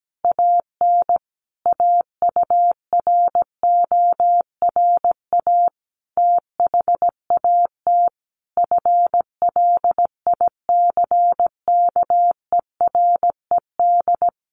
Morse code, heard as a keyed tone, about 17 wpm, 700 Hz. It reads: AN AURORA THAT FLICKERED
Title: A Sound Of Thunder 17wpm